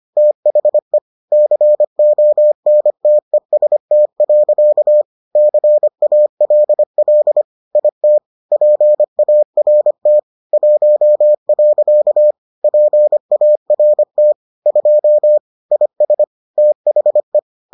Morse code, heard as a keyed tone, about 25 wpm, 600 Hz. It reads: THE CONTEST. CALL IT PART 1. PART 2 IS THE